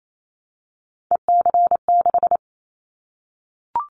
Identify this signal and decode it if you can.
Morse code, heard as a keyed tone, about 28 wpm, 700 Hz. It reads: EC6
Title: Callsign Prefixes for Top 100 DXCC - 28wpm